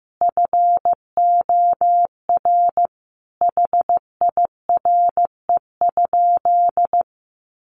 Morse code, heard as a keyed tone, about 15 wpm, 700 Hz. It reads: FOR HIRE?
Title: Sentences from Top 1400 Words 15wpm